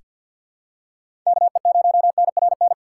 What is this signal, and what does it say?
Morse code, heard as a keyed tone, about 47 wpm, 700 Hz. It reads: KE0NRN
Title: USA Call Signs - 4x Character Spacing - 50wpm